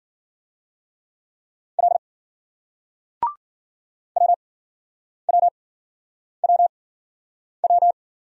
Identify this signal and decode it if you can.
Morse code, heard as a keyed tone, about 50 wpm, spaced as for 9 wpm, 700 Hz. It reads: L WWWW
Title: Speed-Racing Single Letters 40wpm